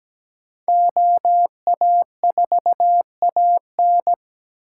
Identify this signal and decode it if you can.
Morse code, heard as a keyed tone, about 20 wpm, 700 Hz. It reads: OA4AN
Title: Speed Racing - USA 2x2 Call Signs 17wpm